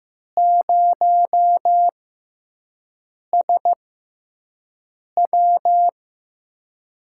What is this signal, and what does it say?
Morse code, heard as a keyed tone, about 15 wpm, spaced as for 5 wpm, 700 Hz. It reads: 0SW
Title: USA Call Signs - 6x Character Spacing - 15wpm